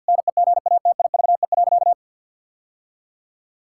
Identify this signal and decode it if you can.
Morse code, heard as a keyed tone, about 50 wpm, 700 Hz. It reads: NEGATIVE.
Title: Sentences from Top 1000 Words - Review Part 6 - 50wpm